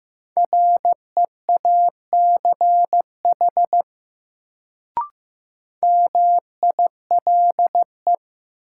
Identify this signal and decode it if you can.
Morse code, heard as a keyed tone, about 15 wpm, 700 Hz. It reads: REACH MILE
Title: Sentences from Top 400 Words - New Word Drill - 15wpm